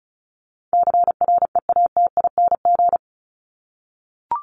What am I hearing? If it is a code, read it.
Morse code, heard as a keyed tone, about 34 wpm, 700 Hz. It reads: CREATING
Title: ING Suffix - Top 100 Words with Base Words 35wpm